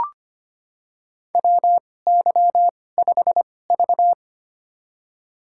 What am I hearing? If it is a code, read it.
Morse code, heard as a keyed tone, about 25 wpm, spaced as for 18 wpm, 700 Hz. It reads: WY5V
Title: USA Call Signs - 4 Characters - 2x Character Spacing - 25wpm